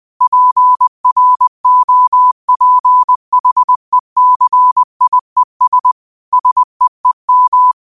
Morse code, heard as a keyed tone, about 20 wpm, 1000 Hz. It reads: PROPHECIES SEEM
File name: qotdfiles / 20wpm